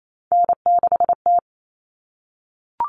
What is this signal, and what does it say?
Morse code, heard as a keyed tone, about 28 wpm, 700 Hz. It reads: N6T
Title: All Cut Numbers - 3-Digit 28wpm